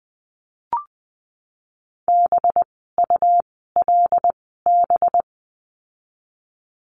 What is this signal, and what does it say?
Morse code, heard as a keyed tone, about 19 wpm, 700 Hz. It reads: B U L B
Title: 185 Lesson 30f - Letter b - Words 20wpm